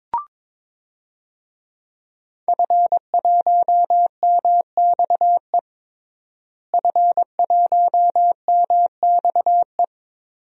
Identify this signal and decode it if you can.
Morse code, heard as a keyed tone, about 22 wpm, 700 Hz. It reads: F1MXE F1MXE
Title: DXCC-100 Call Signs - 5 Characters - Repeated 3x - 22wpm